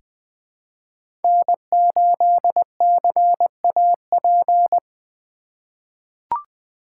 Morse code, heard as a keyed tone, about 20 wpm, 700 Hz. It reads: N8CAP
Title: USA Call Signs - 4x Character Spacing - 20wpm